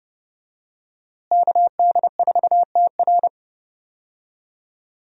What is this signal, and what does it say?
Morse code, heard as a keyed tone, about 30 wpm, 700 Hz. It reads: KD4TR
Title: USA Call Signs - 5 Characters - Repeated 3x - 30wpm